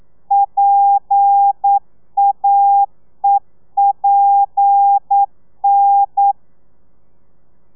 Morse code, heard as a keyed tone, about 9 wpm, 800 Hz. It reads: PAEPN
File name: telegrafia / i2tfj / Lez7 40